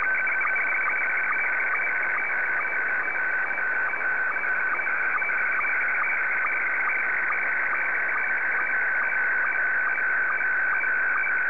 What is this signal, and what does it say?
SSTV (slow-scan television), Scottie 1.